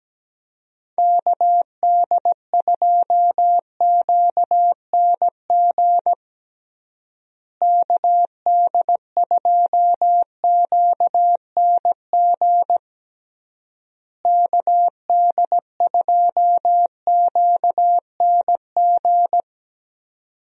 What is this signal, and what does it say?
Morse code, heard as a keyed tone, about 17 wpm, 700 Hz. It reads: KD2QNG KD2QNG KD2QNG
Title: USA Call Signs - 6 Characters - Repeated 3x - 17wpm